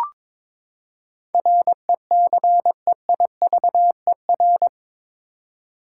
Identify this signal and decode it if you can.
Morse code, heard as a keyed tone, about 22 wpm, 700 Hz. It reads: RECEIVER
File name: practice / 22wpm / Eight Letter Words - 22wpm